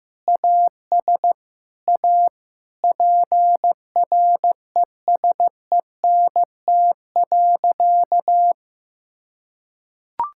Morse code, heard as a keyed tone, about 15 wpm, 700 Hz. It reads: AS A PRESENT.
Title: Sentences from Top 1000 Words - Review Part 5 - 15wpm